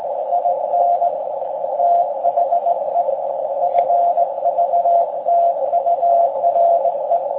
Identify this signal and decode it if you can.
Morse code, heard as a keyed tone, about 18 wpm, 690 Hz, with radio noise. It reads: IN THE FVTURE
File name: forum / im / aq grimeton